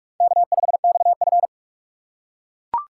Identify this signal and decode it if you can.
Morse code, heard as a keyed tone, about 44 wpm, 700 Hz. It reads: K5XF